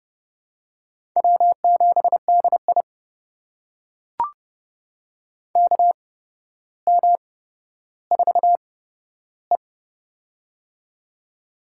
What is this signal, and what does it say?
Morse code, heard as a keyed tone, about 30 wpm, spaced as for 13 wpm, 700 Hz. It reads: W7DI KM4E
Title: USA Call Signs - 4 Characters - 8x Character Spacing - 30wpm